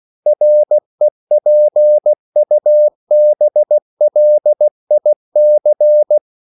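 Morse code, heard as a keyed tone, about 16 wpm, 600 Hz. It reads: REPUBLIC
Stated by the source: CW practice of national capitals at 16 words per minute, from ON6ZQ.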